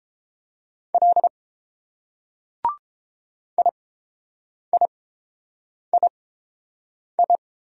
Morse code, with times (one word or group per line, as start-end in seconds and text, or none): 0.9-1.3 L
3.6-7.4 IIII